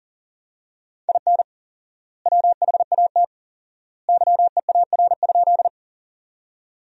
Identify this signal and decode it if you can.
Morse code, heard as a keyed tone, about 40 wpm, 700 Hz. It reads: IN WHAT YEAR?
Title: 4x Word-Spacing - Sentences from Top 200 Words - 40wpm